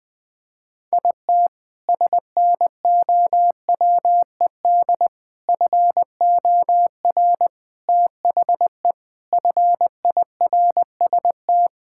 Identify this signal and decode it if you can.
Morse code, heard as a keyed tone, about 21 wpm, 700 Hz. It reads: IT SNOWED FOR THE FIRST
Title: Sentences with 6-Word Phrases - Spoken Beginning - 20wpm